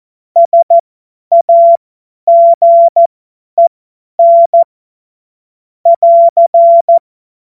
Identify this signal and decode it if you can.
Morse code, heard as a keyed tone, about 14 wpm, spaced as for 10 wpm, 680 Hz. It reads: SAGEN <AR>
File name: Witz-28-Oktober-2025-50-F70